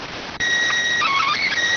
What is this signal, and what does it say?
SSTV (slow-scan television), PD 290.